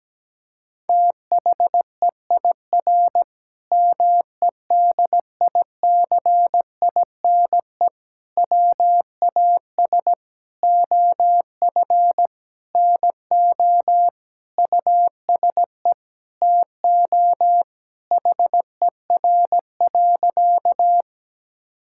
Morse code, heard as a keyed tone, about 17 wpm, 700 Hz. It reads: THEIR MEDICINE WAS OF NO USE TO HER.
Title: Sentences from Top 1000 Words - Review Part 6 - 17wpm